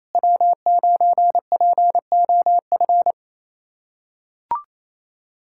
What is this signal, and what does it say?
Morse code, heard as a keyed tone, about 28 wpm, 700 Hz. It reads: W9POF
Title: USA Callsign DE Callsign - 5 Characters - 28wpm